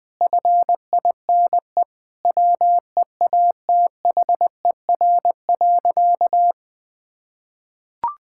Morse code, heard as a keyed tone, about 21 wpm, 700 Hz. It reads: FINE WEATHER.
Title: Sentences from Top 1000 Words - Review Part 5 - 20wpm